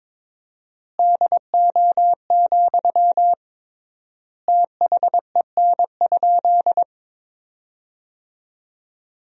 Morse code, heard as a keyed tone, about 21 wpm, 700 Hz. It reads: DO, THEN?
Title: 3x Word-Spacing - Sentences from Top 100 Words - 22wpm